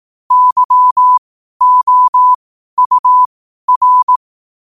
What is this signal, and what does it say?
Morse code, heard as a keyed tone, about 18 wpm, spaced as for 12 wpm, 1000 Hz. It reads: YOUR
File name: qotdfiles / 12wpm